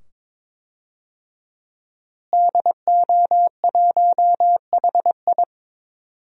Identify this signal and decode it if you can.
Morse code, heard as a keyed tone, about 22 wpm, 700 Hz. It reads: DO1HI
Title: DXCC-100 Call Signs - 5 Characters - Repeated 2x - 22wpm